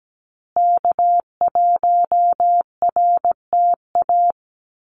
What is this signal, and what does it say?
Morse code, heard as a keyed tone, about 17 wpm, 700 Hz. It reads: K1RTA
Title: USA Call Signs Repeated 2x 17wpm